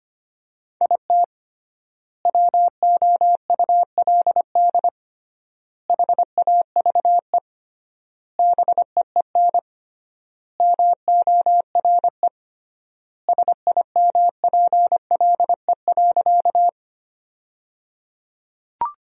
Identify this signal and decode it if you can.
Morse code, heard as a keyed tone, about 25 wpm, 700 Hz. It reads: IT WOULD HAVE BEEN MORE SIMPLE.
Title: 3x Word-Spacing - Sentences from Top 400 Words - 25wpm